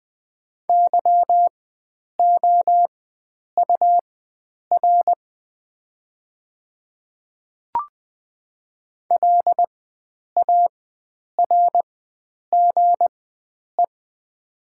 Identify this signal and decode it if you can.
Morse code, heard as a keyed tone, about 20 wpm, spaced as for 9 wpm, 700 Hz. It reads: YOUR LARGE
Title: 4x Character Spacing - Top 300 Words - 20wpm